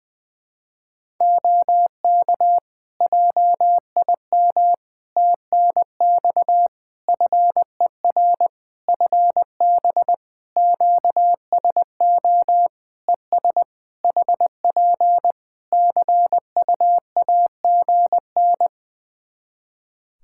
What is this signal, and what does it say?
Morse code, heard as a keyed tone, about 20 wpm, 700 Hz. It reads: OK JIM TNX FER FB QSO ES HP CU AGN
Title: Practice QSO - Exchange 4 - Ending - 20wpm